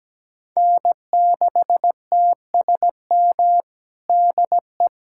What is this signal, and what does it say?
Morse code, heard as a keyed tone, about 17 wpm, 700 Hz. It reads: N6TSM DE
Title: USA Callsign DE Callsign - 5 Characters - 17wpm